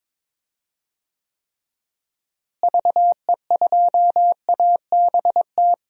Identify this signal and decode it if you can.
Morse code, heard as a keyed tone, about 21 wpm, 700 Hz. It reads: VE2ABT
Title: DXCC-100 Call Signs - 6 Characters - Repeated 3x - 22wpm